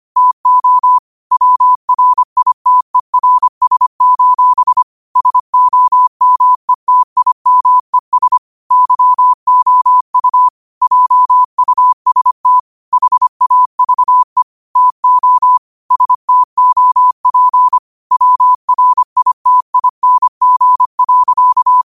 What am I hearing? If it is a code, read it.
Morse code, heard as a keyed tone, about 25 wpm, 1000 Hz. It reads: TO WRITERS: SOMETIMES YOU JUST HAVE TO STOP WRITING.
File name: qotdfiles / 25wpm